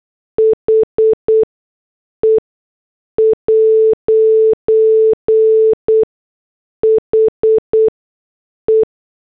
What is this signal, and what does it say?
Morse code, heard as a keyed tone, about 8 wpm, spaced as for 6 wpm, 440 Hz. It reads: HE'HE
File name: morse / headlines